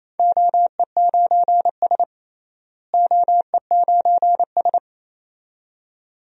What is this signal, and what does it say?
Morse code, heard as a keyed tone, about 28 wpm, 700 Hz. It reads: OE9S OE9S
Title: DXCC-100 Call Signs - 4 Characters - Repeated 2x - 28wpm